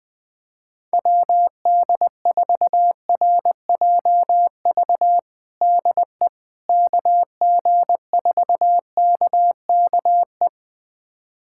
Morse code, heard as a keyed tone, about 21 wpm, 700 Hz. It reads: WD4RJV DE KG4KKE
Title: USA Callsign DE Callsign - 6 Characters - 20wpm